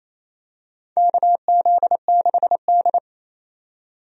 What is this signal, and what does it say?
Morse code, heard as a keyed tone, about 27 wpm, 700 Hz. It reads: KZ6D
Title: USA Call Signs - 4 Characters - 28wpm